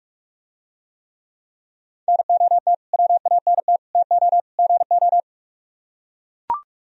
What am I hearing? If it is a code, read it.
Morse code, heard as a keyed tone, about 44 wpm, 700 Hz. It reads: NOT WANT TO GO